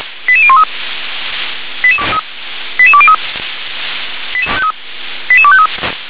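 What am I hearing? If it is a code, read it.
Five-tone selective call (ZVEI), heard as quick runs of five tones, tones of about 70 ms. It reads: 88013 88203 88253
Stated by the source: ZVEI five-tone call 88047 received by SDR.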